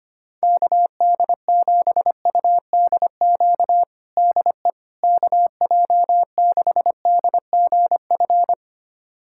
Morse code, heard as a keyed tone, about 25 wpm, 700 Hz. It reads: KD7UDQ DE KJ6DGF
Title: USA Callsign DE Callsign - 6 Characters - 25wpm